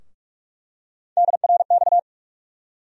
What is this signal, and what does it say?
Morse code, heard as a keyed tone, about 46 wpm, 700 Hz. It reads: DRX
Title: USA Call Sign Postfixes - 3 Letters 45wpm